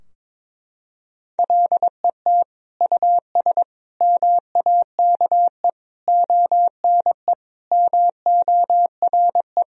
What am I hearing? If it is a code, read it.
Morse code, heard as a keyed tone, about 22 wpm, 700 Hz. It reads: LET US MAKE ONE MORE
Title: Sentences from Top 1000 Words - Review Part 4 - 22wpm